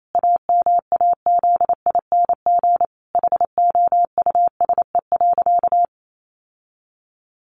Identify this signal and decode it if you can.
Morse code, heard as a keyed tone, about 28 wpm, 700 Hz. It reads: AMAZING HOUSE.
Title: Sentences from Top 1500 Words 28wpm